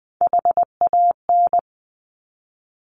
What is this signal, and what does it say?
Morse code, heard as a keyed tone, about 20 wpm, 700 Hz. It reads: HAN
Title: Top Three Letter Combinations in Words 20wpm